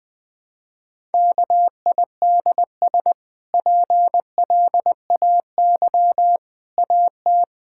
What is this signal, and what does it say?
Morse code, heard as a keyed tone, about 20 wpm, 700 Hz. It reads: KIDS PLAY AT
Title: Easy Sentences with Hints - 20wpm